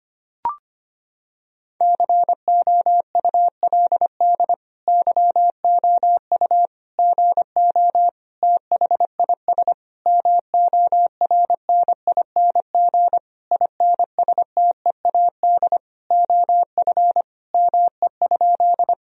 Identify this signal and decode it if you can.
Morse code, heard as a keyed tone, about 25 wpm, 700 Hz. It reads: COULD YOU GO THIS MORNING INSTEAD OF ME?
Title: Sentences from Top 1000 Words - Review Part 4 - 25wpm